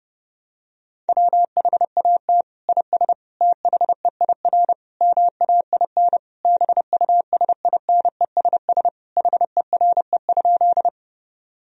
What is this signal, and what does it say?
Morse code, heard as a keyed tone, about 30 wpm, 700 Hz. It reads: WHAT IS THEIR MAIN BUSINESS HERE?
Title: Sentences from Top 1000 Words - Review Part 2 - 30wpm